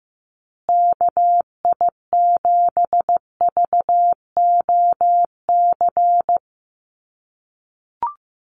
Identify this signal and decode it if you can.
Morse code, heard as a keyed tone, about 15 wpm, 700 Hz. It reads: KI7VOC
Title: USA Call Signs Repeated 2x 15wpm